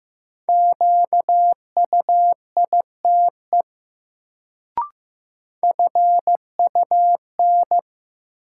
Morse code, heard as a keyed tone, about 15 wpm, 700 Hz. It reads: QUITE FUN
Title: Sentences from Top 700 Words - New Word Drill - 15wpm